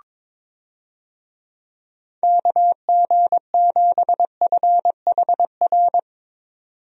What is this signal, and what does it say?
Morse code, heard as a keyed tone, about 22 wpm, 700 Hz. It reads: KG7FHR
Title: USA Call Signs - 6 Characters - 22wpm